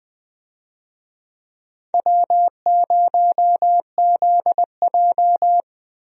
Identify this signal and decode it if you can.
Morse code, heard as a keyed tone, about 20 wpm, 700 Hz. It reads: W0ZJ
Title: USA Call Signs - 4 Characters - 20wpm